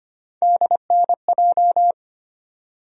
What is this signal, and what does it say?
Morse code, heard as a keyed tone, about 25 wpm, 700 Hz. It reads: DNJ
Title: USA Call Sign Postfixes - 3 Letters 25wpm